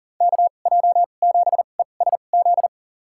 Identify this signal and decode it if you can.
Morse code, heard as a keyed tone, about 40 wpm, spaced as for 29 wpm, 700 Hz. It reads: KJ7ESZ
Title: USA Call Signs - 2x Character Spacing - 40wpm